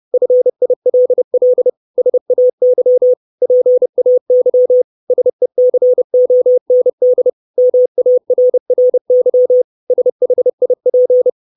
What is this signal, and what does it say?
Morse code, heard as a keyed tone, about 30 wpm, 500 Hz. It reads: FILL SAY PAY SECOND MARRY SHIP
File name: 632 Words 30WPM